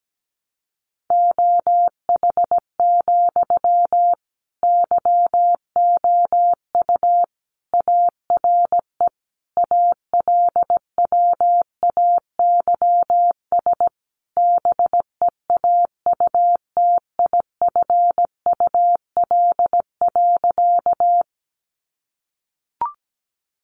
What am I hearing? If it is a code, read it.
Morse code, heard as a keyed tone, about 17 wpm, 700 Hz. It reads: OH, YOU ARE ALWAYS BEAUTIFUL.
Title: Sentences from Top 900 Words 17wpm